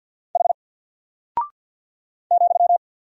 Morse code, heard as a keyed tone, about 48 wpm, 700 Hz. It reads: H ,
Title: Single Character (Letter:Number:Punctuation:Prosign) Alt 50wpm